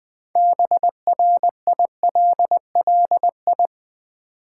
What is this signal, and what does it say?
Morse code, heard as a keyed tone, about 20 wpm, 700 Hz. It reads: BRILLI
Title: Build A Word - 3 Letters At A Time - 9-Letter Words - 20wpm